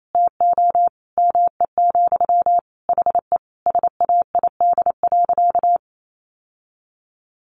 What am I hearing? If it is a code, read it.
Morse code, heard as a keyed tone, about 28 wpm, 700 Hz. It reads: TO ME, HE SAID.
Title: Sentences from Top 600 Words 28wpm